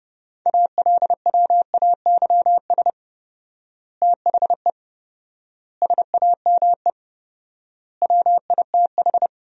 Morse code, heard as a keyed tone, about 30 wpm, 700 Hz. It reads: ALWAYS THE SAME WITH
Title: 4x Word-Spacing - Sentences from Top 300 Words - 30wpm